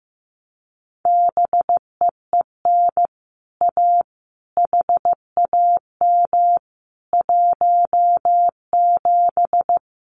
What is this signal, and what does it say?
Morse code, heard as a keyed tone, about 15 wpm, 700 Hz. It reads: BEEN A HAM 17
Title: Common QSO Phrases Encore 15wpm